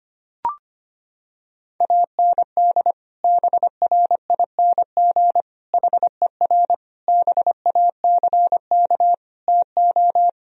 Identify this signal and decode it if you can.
Morse code, heard as a keyed tone, about 25 wpm, 700 Hz. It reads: AND BRING HER BACK TO
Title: Sentences from Top 1000 Words - Review Part 3 - 25wpm